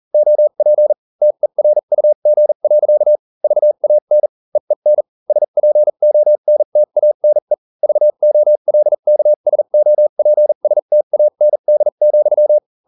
Morse code, heard as a keyed tone, about 40 wpm, spaced as for 35 wpm, 600 Hz. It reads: OP TERUG. VAN EEN SPONTANE VOLKSOPSTAND,